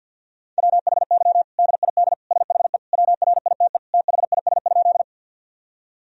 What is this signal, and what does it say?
Morse code, heard as a keyed tone, about 52 wpm, 700 Hz. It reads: WHY DID SHE WRITE THIS?
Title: Sentences from Top 1000 Words - Review Part 1 - 50wpm